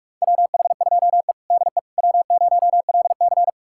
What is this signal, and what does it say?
Morse code, heard as a keyed tone, about 46 wpm, 700 Hz. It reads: WH2E DE W0LC
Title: USA Callsign DE Callsign - 4 Characters - 45wpm